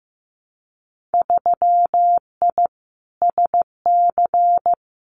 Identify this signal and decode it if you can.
Morse code, heard as a keyed tone, about 15 wpm, 700 Hz. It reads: 3I SC
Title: Winter Field Day Exchanges Encore 15wpm